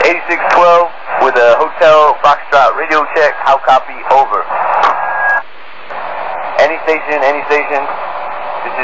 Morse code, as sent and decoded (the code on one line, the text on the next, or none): none